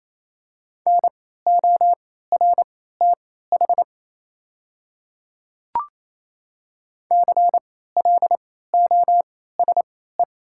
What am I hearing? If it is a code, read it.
Morse code, heard as a keyed tone, about 28 wpm, spaced as for 16 wpm, 690 Hz. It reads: NORTH CLOSE
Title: 3x Character Spacing - Top 300 Words - 28wpm